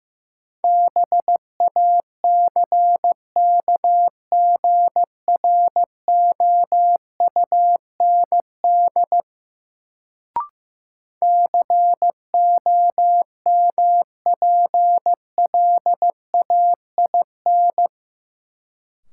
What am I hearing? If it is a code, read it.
Morse code, heard as a keyed tone, about 15 wpm, 700 Hz. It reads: BACKGROUND COMPLAIN
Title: Sentences from Top 1300 Words - New Word Drill Alt - 15wpm